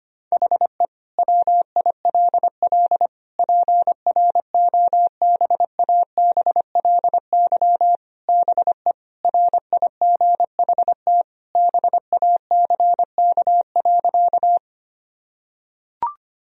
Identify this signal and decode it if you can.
Morse code, heard as a keyed tone, about 25 wpm, 700 Hz. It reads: HE WILL PROBABLY BE RIGHT BACK.
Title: Sentences from Top 1000 Words - Review Part 4 - 25wpm